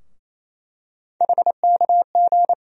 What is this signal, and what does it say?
Morse code, heard as a keyed tone, about 28 wpm, 700 Hz. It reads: HKG
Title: USA Call Sign Postfixes - 3 Letters 28wpm